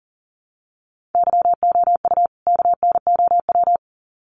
Morse code, heard as a keyed tone, about 40 wpm, 700 Hz. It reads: YOU KNOW
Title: Sets of 2 Words 40wpm